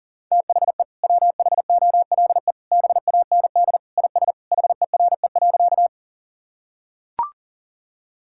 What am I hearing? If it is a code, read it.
Morse code, heard as a keyed tone, about 37 wpm, 700 Hz. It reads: THE WHOLE BAND IS HERE.
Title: Sentences from Top 1600 Words 40wpm